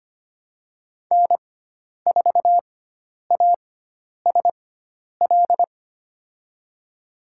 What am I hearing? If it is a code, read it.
Morse code, heard as a keyed tone, about 25 wpm, spaced as for 10 wpm, 700 Hz. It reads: N4ASL
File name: Lesson 16g - Letter c - Call Signs 25wpm